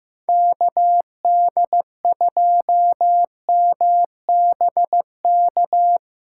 Morse code, heard as a keyed tone, about 15 wpm, 700 Hz. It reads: KD2MBK
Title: USA Callsign DE Callsign 15wpm